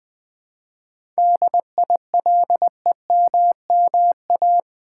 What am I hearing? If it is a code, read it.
Morse code, heard as a keyed tone, about 20 wpm, 700 Hz. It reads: DILEMMA
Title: 7 Letter Words - First 3 Letters Given - 20wpm